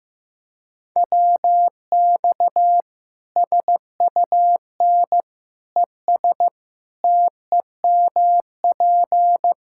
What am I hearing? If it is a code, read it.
Morse code, heard as a keyed tone, about 15 wpm, 700 Hz. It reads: WX SUN ES TEMP
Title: Practice QSO - Exchange 2 - Info - 15wpm